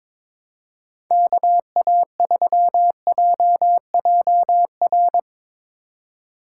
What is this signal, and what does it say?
Morse code, heard as a keyed tone, about 22 wpm, 700 Hz. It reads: KA3JJR
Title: USA Call Signs - 6 Characters - 2x Character Spacing - 22wpm